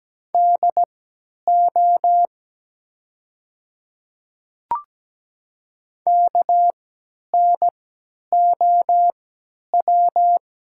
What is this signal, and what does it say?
Morse code, heard as a keyed tone, about 17 wpm, spaced as for 10 wpm, 700 Hz. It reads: DO KNOW
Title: 3x Character Spacing - Top 100 Words - 17wpm